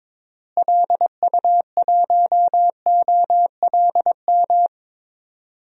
Morse code, heard as a keyed tone, about 22 wpm, 700 Hz. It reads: LU1OLM
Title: DXCC-100 Call Signs - 6 Characters - Repeated 3x - 22wpm